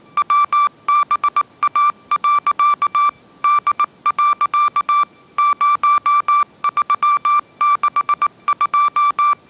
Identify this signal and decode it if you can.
Morse code, heard as a keyed tone, about 19 wpm, 1170 Hz. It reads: WBA. D. 0362
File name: 20wpm